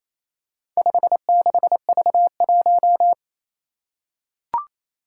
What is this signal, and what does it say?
Morse code, heard as a keyed tone, about 28 wpm, 700 Hz. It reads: 56V1
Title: Four Challenging Letter-Number Combinations 28wpm